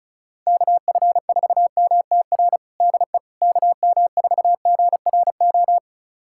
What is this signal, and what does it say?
Morse code, heard as a keyed tone, about 36 wpm, 700 Hz. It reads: KF4MTR DE KM4GRO
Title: USA Callsign DE Callsign 35wpm